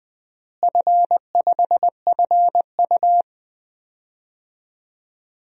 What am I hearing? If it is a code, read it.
Morse code, heard as a keyed tone, about 21 wpm, 700 Hz. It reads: F5FU
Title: DXCC-100 Call Signs - 4 Characters - Repeated 2x - 20wpm